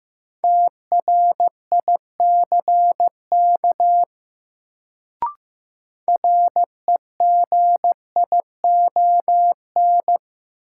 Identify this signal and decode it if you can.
Morse code, heard as a keyed tone, about 15 wpm, 700 Hz. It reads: TRICK REGION
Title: Sentences from Top 1500 Words - New Word Drill - 15wpm